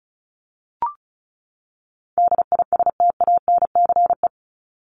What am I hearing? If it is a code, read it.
Morse code, heard as a keyed tone, about 34 wpm, 700 Hz. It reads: DISTANCE
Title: Top 1001-1250 Words 35wpm